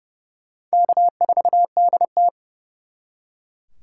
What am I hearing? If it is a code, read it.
Morse code, heard as a keyed tone, about 36 wpm, 700 Hz. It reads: K4DT
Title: Speed Racing - USA 1x2 Call Signs 30wpm